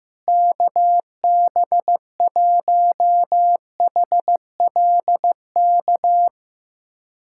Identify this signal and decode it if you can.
Morse code, heard as a keyed tone, about 16 wpm, 700 Hz. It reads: KB1HLK
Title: USA Callsign DE Callsign - 6 Characters - 15wpm